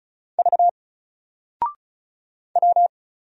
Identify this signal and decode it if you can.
Morse code, heard as a keyed tone, about 34 wpm, 700 Hz. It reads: V W
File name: Single-Letter-Number-35wpm